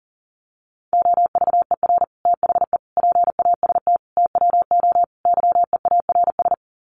Morse code, heard as a keyed tone, about 40 wpm, 700 Hz. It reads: OVER THE PAST TWO YEARS
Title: Sets of 5 Words Encore 40wpm